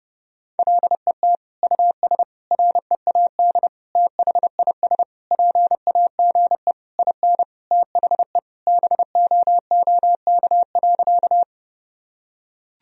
Morse code, heard as a keyed tone, about 30 wpm, 700 Hz. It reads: LET US READ THIS PAGE IN THE BOOK.
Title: Sentences from Top 1000 Words - Review Part 1 - 30wpm